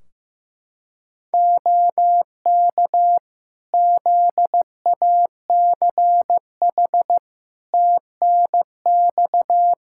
Morse code, heard as a keyed tone, about 15 wpm, 700 Hz. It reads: OK ZACH TNX
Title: K1USN SST QSO Practice 15wpm